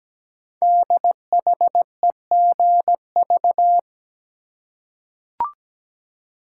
Morse code, heard as a keyed tone, about 17 wpm, 700 Hz. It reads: DHEGV
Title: Code Groups - 5 Letters - 17wpm